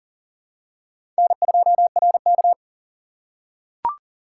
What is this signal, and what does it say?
Morse code, heard as a keyed tone, about 40 wpm, 700 Hz. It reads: N2RK
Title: USA Call Signs - 4 Characters - 4x Character Spacing - 40wpm